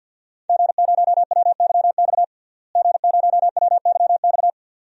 Morse code, heard as a keyed tone, about 46 wpm, 700 Hz. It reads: G0WYX G0WYX
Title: DXCC-100 Call Signs - 5 Characters - Repeated 2x - 50wpm